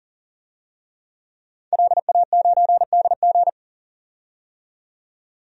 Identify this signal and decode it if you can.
Morse code, heard as a keyed tone, about 37 wpm, 700 Hz. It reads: LA9DG
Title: DXCC-100 Call Signs - 5 Characters - 40wpm